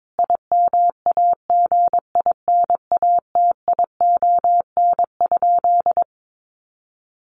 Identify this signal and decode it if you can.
Morse code, heard as a keyed tone, about 22 wpm, 700 Hz. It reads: IMAGINATION?
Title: Sentences from Top 1300 Words 22wpm